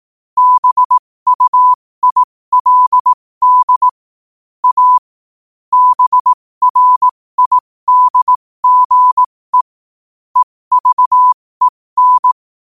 Morse code, heard as a keyed tone, about 18 wpm, spaced as for 15 wpm, 1000 Hz. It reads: BUILD A BRIDGE EVEN
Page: New quotes every day in morse code at 15 Words per minute.